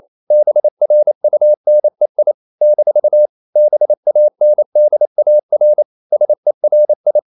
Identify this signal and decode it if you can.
Morse code, heard as a keyed tone, about 28 wpm, 600 Hz. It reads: BRUNEI - BANDAR SERI
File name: CW / Practice / NationalCapitals28